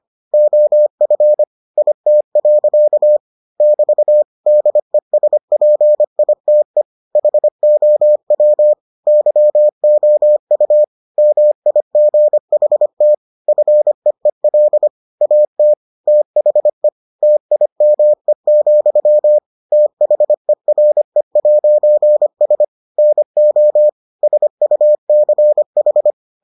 Morse code, heard as a keyed tone, about 25 wpm, 600 Hz. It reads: OF IT. <BT> DESPITE HOW YOU MIGHT FEEL AT THE TIME, THERE'S NO SUCH